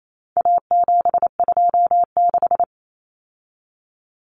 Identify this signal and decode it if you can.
Morse code, heard as a keyed tone, about 28 wpm, 700 Hz. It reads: A726
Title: Common Cut Numbers - 4-Digit 28wpm